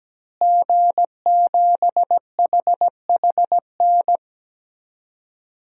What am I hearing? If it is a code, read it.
Morse code, heard as a keyed tone, about 17 wpm, 700 Hz. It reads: G7HHN
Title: DXCC-100 Call Signs - 5 Characters - Repeated 2x - 17wpm